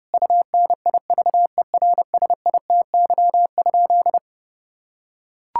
Morse code, heard as a keyed tone, about 30 wpm, 700 Hz. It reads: UNIVERSITY?
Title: Sentences from Top 1000 Words - Review Part 6 - 30wpm